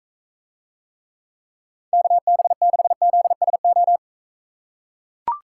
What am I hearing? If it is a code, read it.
Morse code, heard as a keyed tone, about 43 wpm, 700 Hz. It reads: KB6ZSO